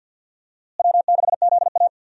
Morse code, heard as a keyed tone, about 52 wpm, 700 Hz. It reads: W6ZA
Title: USA Call Signs - 4 Characters - Repeated 2x - 50wpm